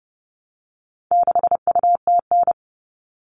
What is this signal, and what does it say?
Morse code, heard as a keyed tone, about 30 wpm, 700 Hz. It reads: 6UTN
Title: All Cut Numbers - 4-Digit 30wpm